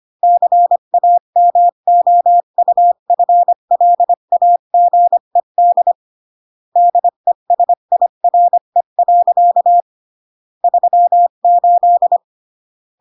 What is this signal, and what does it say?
Morse code, heard as a keyed tone, about 25 wpm, 700 Hz. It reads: CAMOUFLAGED DESIRE. 38